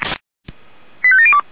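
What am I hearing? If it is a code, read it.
Five-tone selective call (ZVEI), heard as quick runs of five tones, tones of about 70 ms. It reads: 85702
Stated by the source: ZVEI five-tone call received with an SDR, from the Feuerwehr Orlamünde audio server.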